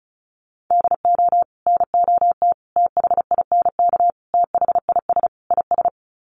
Morse code, heard as a keyed tone, about 34 wpm, 700 Hz. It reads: DO NOT THINK THIS IS
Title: Sets of 5 Words Encore2 35wpm